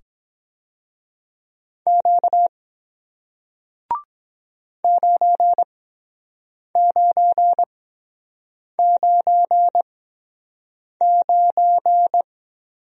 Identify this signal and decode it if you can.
Morse code, heard as a keyed tone, about 23 wpm, spaced as for 7 wpm, 700 Hz. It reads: Q 9999